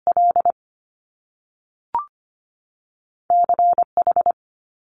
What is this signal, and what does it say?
Morse code, heard as a keyed tone, about 25 wpm, 700 Hz. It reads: L C H
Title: Top Two Letter Combinations in Words 25wpm